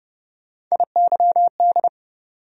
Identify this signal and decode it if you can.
Morse code, heard as a keyed tone, about 32 wpm, 700 Hz. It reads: IYD
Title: USA Call Sign Postfixes - 3 Letters 30wpm